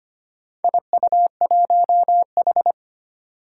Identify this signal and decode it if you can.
Morse code, heard as a keyed tone, about 25 wpm, 700 Hz. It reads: IU1H
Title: Four Challenging Letter-Number Combinations 25wpm